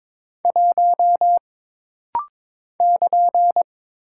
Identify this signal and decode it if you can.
Morse code, heard as a keyed tone, about 22 wpm, 700 Hz. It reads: 1 <KN>
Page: This rapid-fire timing is geared to push your ICR (Instant Character Recognition) skills.